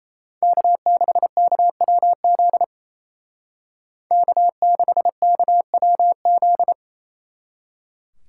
Morse code, heard as a keyed tone, about 34 wpm, 700 Hz. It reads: K6KWZ K6KWZ
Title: Speed Racing - USA 1x3 Call Signs 28wpm